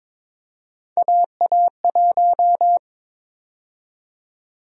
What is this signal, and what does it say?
Morse code, heard as a keyed tone, about 22 wpm, 700 Hz. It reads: AA1
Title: USA Call Sign Prefixes - 22wpm